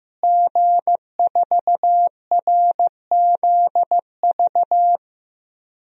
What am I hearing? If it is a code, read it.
Morse code, heard as a keyed tone, about 15 wpm, 700 Hz. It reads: G4RZV
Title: DXCC-100 Call Signs - 5 Characters - Repeated 3x - 15wpm